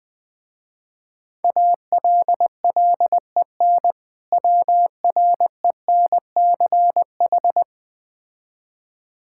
Morse code, heard as a keyed tone, about 20 wpm, 700 Hz. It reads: ALLEN WRENCH
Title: Difficult Words with Hints - 20wpm